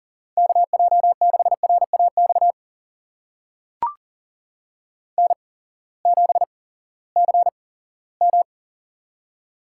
Morse code, heard as a keyed tone, about 37 wpm, spaced as for 19 wpm, 700 Hz. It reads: KJ6RAX N7CM
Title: USA Call Signs - 8x Character Spacing - 40wpm